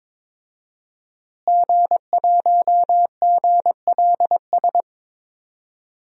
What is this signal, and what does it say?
Morse code, heard as a keyed tone, about 21 wpm, 700 Hz. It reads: G1GLS
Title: DXCC-100 Call Signs - 5 Characters - Repeated 2x - 22wpm